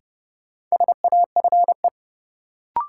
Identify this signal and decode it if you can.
Morse code, heard as a keyed tone, about 30 wpm, 700 Hz. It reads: SAFE
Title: Sentences from Top 900 Words - New Word Drill - 30wpm